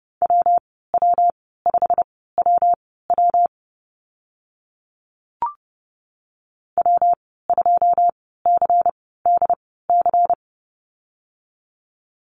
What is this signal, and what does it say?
Morse code, heard as a keyed tone, about 30 wpm, spaced as for 17 wpm, 700 Hz. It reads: WW5WW W2CDC
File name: Lesson 19g - Letter w - Call Signs 30wpm